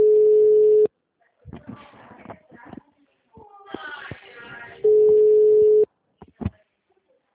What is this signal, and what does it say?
Ringback tone (Europe, 425 Hz), with speech.
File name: out- -2015- - - .8